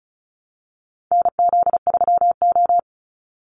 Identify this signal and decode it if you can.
Morse code, heard as a keyed tone, about 34 wpm, 700 Hz. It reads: NZ3O
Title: Call Signs - 35wpm